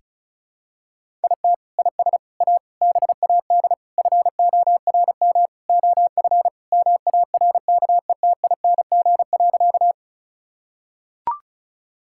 Morse code, heard as a keyed tone, about 36 wpm, 700 Hz. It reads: IT IS A BAD FORM OF MARKETING.
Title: Sentences from Top 1000 Words - Review Part 5 - 35wpm